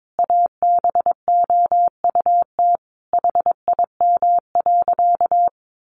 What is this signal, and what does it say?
Morse code, heard as a keyed tone, about 22 wpm, 700 Hz. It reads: ABOUT HIM.
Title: Easy Phrases 22wpm